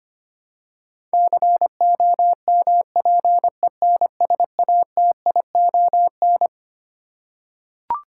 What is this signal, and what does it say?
Morse code, heard as a keyed tone, about 25 wpm, 700 Hz. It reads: COMPENSATION
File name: Twelve Letter Words - 25wpm